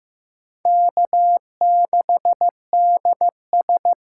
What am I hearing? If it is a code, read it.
Morse code, heard as a keyed tone, about 15 wpm, 700 Hz. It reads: K6DS
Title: USA Call Signs - 4 Characters - 6x Character Spacing - 15wpm